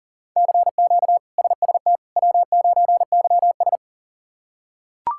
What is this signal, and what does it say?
Morse code, heard as a keyed tone, about 37 wpm, 700 Hz. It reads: CQ SST W9YS
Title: K1USN SST QSO Practice - 40wpm